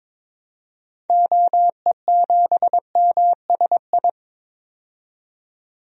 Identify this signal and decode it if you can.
Morse code, heard as a keyed tone, about 22 wpm, 700 Hz. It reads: OE7MSI
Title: DXCC-100 Call Signs - 6 Characters - 22wpm